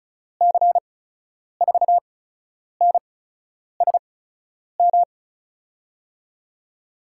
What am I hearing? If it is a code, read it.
Morse code, heard as a keyed tone, about 36 wpm, spaced as for 10 wpm, 700 Hz. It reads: C4NSM
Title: USA Call Signs - 6 Characters - 8x Character Spacing - 35wpm